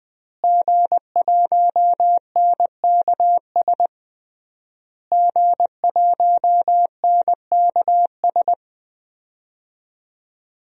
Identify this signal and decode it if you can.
Morse code, heard as a keyed tone, about 21 wpm, 700 Hz. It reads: G1NKS G1NKS
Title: DXCC-100 Call Signs - 5 Characters - Repeated 3x - 20wpm